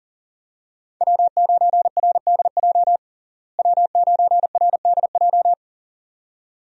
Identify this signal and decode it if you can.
Morse code, heard as a keyed tone, about 40 wpm, 700 Hz. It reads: W9RDJ W9RDJ
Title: USA Call Signs - 5 Characters - Repeated 2x - 40wpm